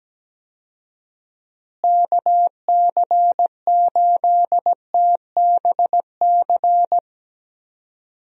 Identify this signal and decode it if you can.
Morse code, heard as a keyed tone, about 17 wpm, 700 Hz. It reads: KC8TBC
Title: USA Call Signs - 6 Characters - Repeated 2x - 17wpm